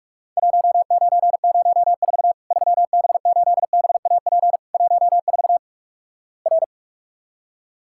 Morse code, heard as a keyed tone, about 46 wpm, 700 Hz. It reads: 1904 3B8BAP 14 R
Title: Worked All Europe DX Contest - QTC Exchange - 45wpm